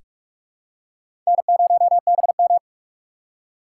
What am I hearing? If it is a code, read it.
Morse code, heard as a keyed tone, about 46 wpm, 690 Hz. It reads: N0BM
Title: USA Call Signs - 4 Characters - 3x Character Spacing - 45wpm